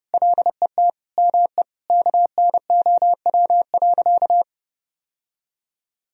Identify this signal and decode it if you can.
Morse code, heard as a keyed tone, about 30 wpm, 700 Hz. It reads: LET ME KNOW.
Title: Sentences from Top 1000 Words - Review Part 2 - 30wpm